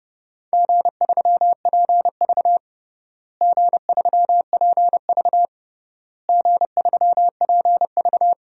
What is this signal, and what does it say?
Morse code, heard as a keyed tone, about 30 wpm, 700 Hz. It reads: G3PV G3PV G3PV
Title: DXCC-100 Call Signs - 4 Characters - Repeated 3x - 30wpm